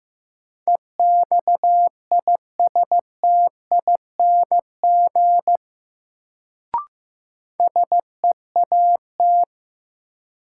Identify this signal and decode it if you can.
Morse code, heard as a keyed tone, about 16 wpm, 700 Hz. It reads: EXISTING SEAT
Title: Sentences from Top 1500 Words - New Word Drill - 15wpm